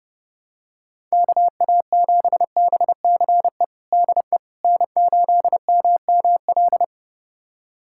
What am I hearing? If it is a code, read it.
Morse code, heard as a keyed tone, about 30 wpm, 700 Hz. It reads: KA7BCE DE N8MML
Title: USA Callsign DE Callsign 30wpm